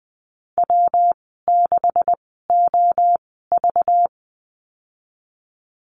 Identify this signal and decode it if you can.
Morse code, heard as a keyed tone, about 20 wpm, spaced as for 15 wpm, 700 Hz. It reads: W6OV
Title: 193 Lesson 31g - Letter v - Call Signs 20wpm